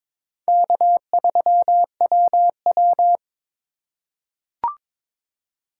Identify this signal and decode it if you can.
Morse code, heard as a keyed tone, about 22 wpm, 700 Hz. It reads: K3WW
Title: USA Call Signs - 2x Character Spacing - 22wpm